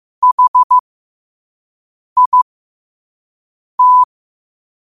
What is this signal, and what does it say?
Morse code, heard as a keyed tone, about 15 wpm, spaced as for 5 wpm, 1000 Hz. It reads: HIT
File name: qotdfiles / 5wpm